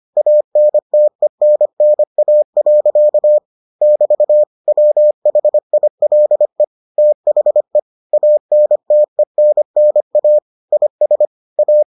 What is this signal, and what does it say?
Morse code, heard as a keyed tone, about 25 wpm, 600 Hz. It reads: ANTENNA. <BT> WHILE THE ANTENNA IS A